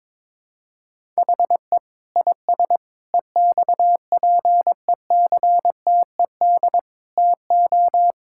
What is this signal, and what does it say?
Morse code, heard as a keyed tone, about 22 wpm, 700 Hz. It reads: HE IS EXPECTED TO
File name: Sentences with 7-Word Phrases - 22wpm